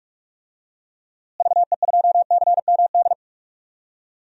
Morse code, heard as a keyed tone, about 46 wpm, 700 Hz. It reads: VE2CMD
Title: DXCC-100 Call Signs - 6 Characters - Repeated 3x - 45wpm